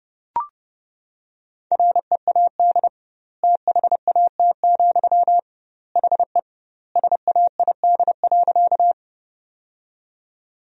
Morse code, heard as a keyed tone, about 32 wpm, 700 Hz. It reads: READ THAT, HE SAID.
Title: 2x Word-Spacing - Sentences from Top 200 Words - 30wpm